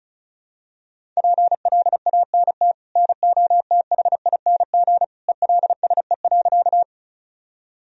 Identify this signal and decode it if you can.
Morse code, heard as a keyed tone, about 34 wpm, 700 Hz. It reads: PLANT NOTHING ELSE.
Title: Sentences from Top 1000 Words - Review Part 1 - 35wpm